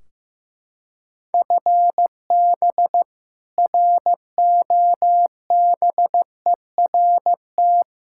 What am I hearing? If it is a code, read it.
Morse code, heard as a keyed tone, about 15 wpm, 700 Hz. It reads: FB ROBERT
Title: K1USN SST QSO Practice 15wpm